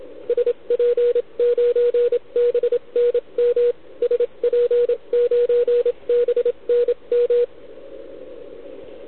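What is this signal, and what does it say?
Morse code, heard as a keyed tone, about 27 wpm, spaced as for 24 wpm, 480 Hz, with radio noise. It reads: SP9BNM SP9BNM
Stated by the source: CW from the OKOM DX contest on 80 m, received with a TS480.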